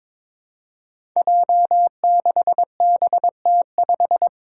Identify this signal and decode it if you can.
Morse code, heard as a keyed tone, about 22 wpm, 700 Hz. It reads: J6BT5
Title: Five Challenging Letter-Number Combinations 22wpm